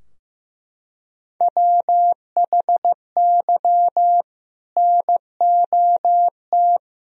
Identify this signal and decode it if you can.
Morse code, heard as a keyed tone, about 15 wpm, 700 Hz. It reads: WHY NOT
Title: Sentences from Top 1000 Words - Review Part 3 - 15wpm